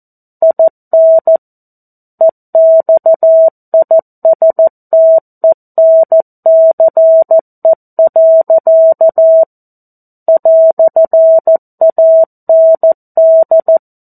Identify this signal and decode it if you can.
Morse code, heard as a keyed tone, about 14 wpm, 650 Hz. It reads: IN EXISTENCE. "AND
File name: 02-Chapter-02 14 WPM